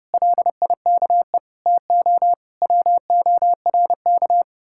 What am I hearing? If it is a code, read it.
Morse code, heard as a keyed tone, about 37 wpm, 700 Hz. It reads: LIKE TO WORK
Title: Speed-Racing Common QSO Phrases - 30wpm